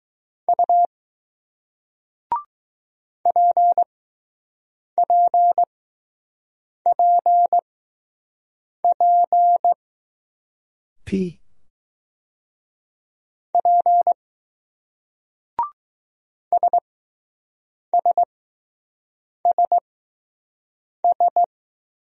Morse code, heard as a keyed tone, about 20 wpm, spaced as for 6 wpm, 700 Hz. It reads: U PPPP P SSSS